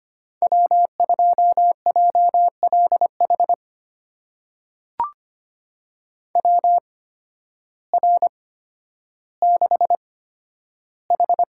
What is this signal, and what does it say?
Morse code, heard as a keyed tone, about 25 wpm, spaced as for 12 wpm, 700 Hz. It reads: W2JLH WR6H
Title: USA Call Signs - 5 Characters - 8x Character Spacing - 25wpm